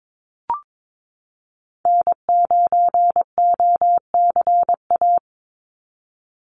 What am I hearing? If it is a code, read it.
Morse code, heard as a keyed tone, about 22 wpm, 700 Hz. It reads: N9OCA
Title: USA Call Signs 22wpm